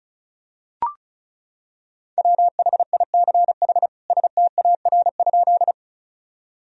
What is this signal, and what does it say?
Morse code, heard as a keyed tone, about 34 wpm, 700 Hz. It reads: WHICH STAR?
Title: Sentences from Top 1000 Words - Review Part 2 - 35wpm